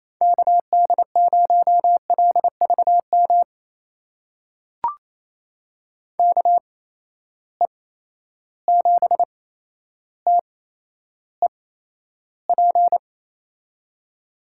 Morse code, heard as a keyed tone, about 28 wpm, spaced as for 12 wpm, 700 Hz. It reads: KD0LVM KE7TEP
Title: USA Call Signs - 6 Characters - 8x Character Spacing - 28wpm